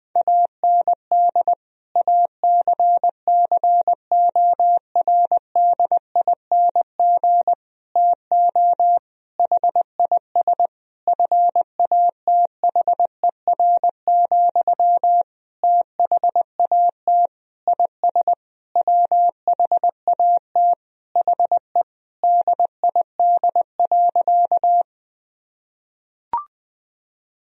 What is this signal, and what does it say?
Morse code, heard as a keyed tone, about 20 wpm, 700 Hz. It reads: AND ACCORDING TO HIS FATHER, THAT IS WHAT HE DID.
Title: Sentences from Top 1000 Words - Review Part 4 - 20wpm